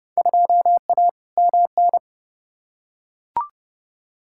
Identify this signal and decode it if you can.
Morse code, heard as a keyed tone, about 30 wpm, 700 Hz. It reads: 2A MN
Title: Field Day Exchanges - 30wpm